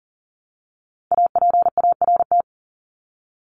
Morse code, heard as a keyed tone, about 40 wpm, 700 Hz. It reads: APART
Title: Top 1501-1750 Words 40wpm